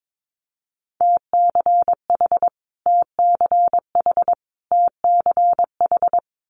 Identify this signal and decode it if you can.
Morse code, heard as a keyed tone, about 22 wpm, 700 Hz. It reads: TCH TCH TCH
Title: Multi-letter Phonemes Repeated 3x 22wpm